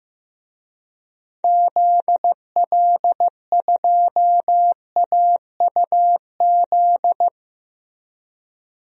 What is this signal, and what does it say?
Morse code, heard as a keyed tone, about 15 wpm, 700 Hz. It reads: ZL2AUZ
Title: DXCC-100 Call Signs - 6 Characters - Repeated 3x - 15wpm